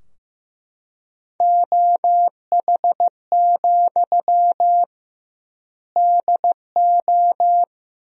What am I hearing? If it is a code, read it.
Morse code, heard as a keyed tone, about 15 wpm, 700 Hz. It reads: OH, DO
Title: 2x Word-Spacing - Sentences from Top 500 Words - 15wpm